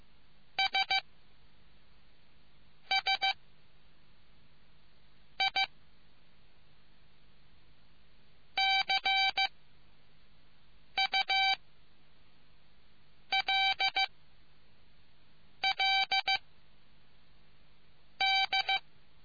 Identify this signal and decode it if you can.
Morse code, heard as a keyed tone, about 15 wpm, 770 Hz. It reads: S S I C U L L D